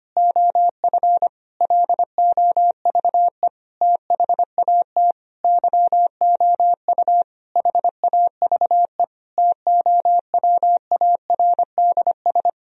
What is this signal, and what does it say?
Morse code, heard as a keyed tone, about 25 wpm, 700 Hz. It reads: OF LOVE THAT YOU HAVE TOWARDS
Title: Sentences from Top 1000 Words - Review Part 4 - 25wpm